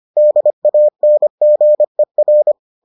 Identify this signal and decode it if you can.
Morse code, heard as a keyed tone, about 25 wpm, 600 Hz. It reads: DANGER